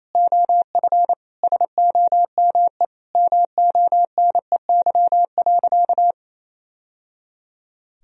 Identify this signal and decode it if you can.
Morse code, heard as a keyed tone, about 28 wpm, 700 Hz. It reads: OF SOME MONEY.
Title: Sentences from Top 1000 Words - Review Part 3 - 28wpm